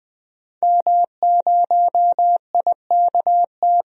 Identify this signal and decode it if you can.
Morse code, heard as a keyed tone, about 21 wpm, 700 Hz. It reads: M0IKT
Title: DXCC-100 Call Signs - 5 Characters - 20wpm